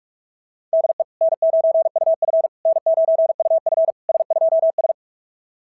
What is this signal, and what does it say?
Morse code, heard as a keyed tone, about 45 wpm, 660 Hz. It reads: DE N9UF N9UF S2S
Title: SOTA QSO Practice 45wpm